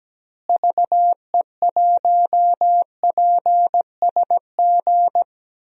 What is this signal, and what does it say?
Morse code, heard as a keyed tone, about 17 wpm, 700 Hz. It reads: VE1PSG
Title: DXCC-100 Call Signs - 6 Characters - Repeated 3x - 17wpm